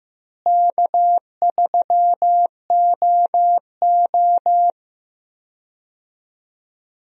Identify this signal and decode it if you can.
Morse code, heard as a keyed tone, about 15 wpm, 700 Hz. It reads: K3OO
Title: USA Call Signs - 4 Characters - Repeated 3x - 15wpm